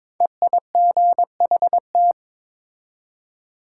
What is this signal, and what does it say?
Morse code, heard as a keyed tone, about 22 wpm, 700 Hz. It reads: EIGHT
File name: 5 Letter Words - First 3 Letters Given - 22wpm